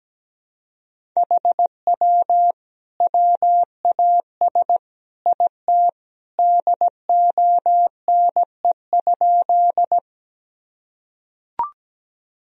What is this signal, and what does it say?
Morse code, heard as a keyed tone, about 17 wpm, 700 Hz. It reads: HW WAS IT DONE?
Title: Sentences from Top 500 Words and Abbreviations 17wpm